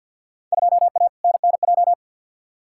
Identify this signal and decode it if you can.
Morse code, heard as a keyed tone, about 53 wpm, 700 Hz. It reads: 2A NNJ
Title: Field Day Exchanges - 50wpm